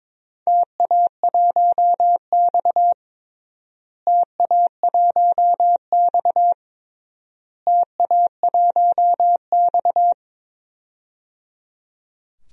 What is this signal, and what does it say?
Morse code, heard as a keyed tone, about 22 wpm, 700 Hz. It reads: TA1X TA1X TA1X
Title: DXCC-100 Call Signs - 4 Characters - Repeated 3x - 22wpm